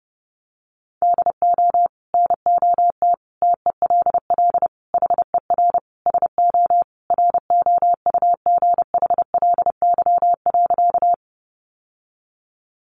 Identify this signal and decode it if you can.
Morse code, heard as a keyed tone, about 30 wpm, 700 Hz. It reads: DO NOT TELL HER SO ROUGHLY.
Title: Sentences from Top 1500 Words 30wpm